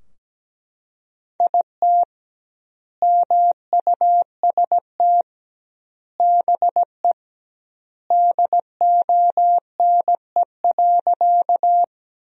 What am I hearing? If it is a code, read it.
Morse code, heard as a keyed tone, about 17 wpm, 700 Hz. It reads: IT MUST BE DONE.
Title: 2x Word-Spacing - Sentences from Top 500 Words - 17wpm